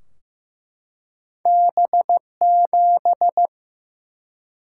Morse code, heard as a keyed tone, about 15 wpm, 700 Hz. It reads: B7
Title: Single-Letter Single-Number 15wpm